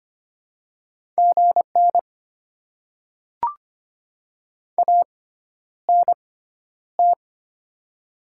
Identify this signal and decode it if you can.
Morse code, heard as a keyed tone, about 25 wpm, spaced as for 11 wpm, 700 Hz. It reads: GN ANT
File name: Top 90 QSO Elements - 6x Character Spacing - 25wpm